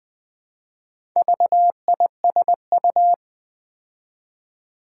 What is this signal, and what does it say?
Morse code, heard as a keyed tone, about 20 wpm, 700 Hz. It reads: VISU
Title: Build A Word - 2 Letters At A Time - 6-Letter Words - 20wpm